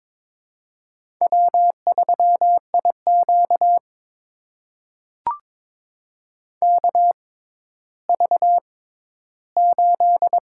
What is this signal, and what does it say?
Morse code, heard as a keyed tone, about 22 wpm, spaced as for 12 wpm, 700 Hz. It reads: W3IQ KV8
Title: USA Call Signs - 4 Characters - 6x Character Spacing - 22wpm